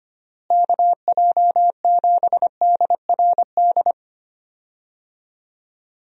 Morse code, heard as a keyed tone, about 25 wpm, 700 Hz. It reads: KJ7DRD
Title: USA Call Signs - 6 Characters - Repeated 2x - 25wpm